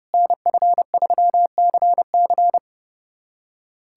Morse code, heard as a keyed tone, about 30 wpm, 700 Hz. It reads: NF3CC
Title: USA Callsign DE Callsign 30wpm